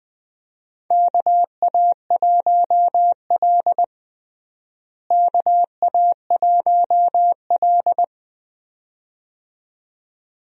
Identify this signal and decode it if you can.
Morse code, heard as a keyed tone, about 20 wpm, 700 Hz. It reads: KA1L KA1L
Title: USA Call Signs - 4 Characters - Repeated 2x - 20wpm